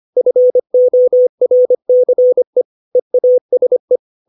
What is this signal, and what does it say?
Morse code, heard as a keyed tone, about 25 wpm, 500 Hz. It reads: FORCE EASE
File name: 632 words / 632 Words 25WPM